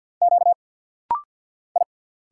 Morse code, heard as a keyed tone, about 50 wpm, 700 Hz. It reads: Q I
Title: Single Letter-Number Alt - Rapid-Fire 50wpm